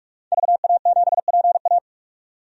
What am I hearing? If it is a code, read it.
Morse code, heard as a keyed tone, about 43 wpm, 700 Hz. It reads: VA7PA
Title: DXCC-100 Call Signs - 5 Characters - 45wpm